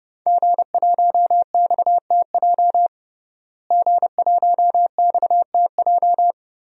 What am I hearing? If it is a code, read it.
Morse code, heard as a keyed tone, about 30 wpm, 700 Hz. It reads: G1XTJ G1XTJ
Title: DXCC-100 Call Signs - 5 Characters - Repeated 3x - 30wpm